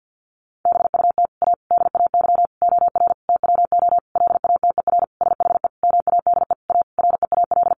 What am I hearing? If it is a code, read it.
Morse code, heard as a keyed tone, about 48 wpm, 700 Hz. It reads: BUT A DAY OR TWO LATER SHE MADE A REAL
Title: Sentences from Top 1100 Words 50wpm